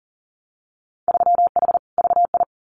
Morse code, heard as a keyed tone, about 40 wpm, 700 Hz. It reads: 3H VI
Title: Winter Field Day Exchanges Encore 40wpm